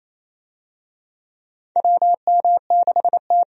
Morse code, heard as a keyed tone, about 28 wpm, 700 Hz. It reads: WM6T
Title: USA Call Signs - 4 Characters - 28wpm